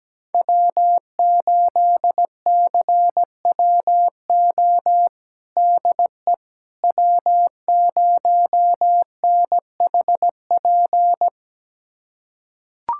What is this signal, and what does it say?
Morse code, heard as a keyed tone, about 17 wpm, 700 Hz. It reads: W8CWO DE W0NHP
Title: USA Callsign DE Callsign - 5 Characters - 17wpm